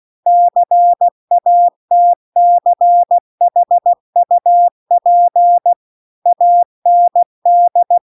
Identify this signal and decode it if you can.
Morse code, heard as a keyed tone, about 16 wpm, 700 Hz. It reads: CATCHUP AND